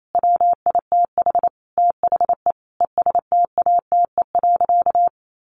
Morse code, heard as a keyed tone, about 28 wpm, 700 Hz. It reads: WITH THE ESTATE.
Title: Sentences from Top 1200 Words 28wpm